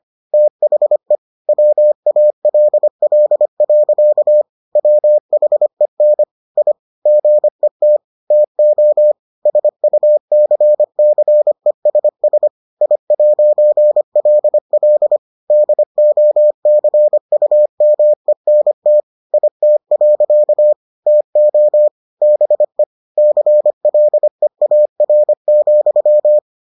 Morse code, heard as a keyed tone, about 25 wpm, 600 Hz. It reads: THE WALL. WHEN I GET TO SUCCESS I'LL DOCUMENT IT. TO BE CLEAR,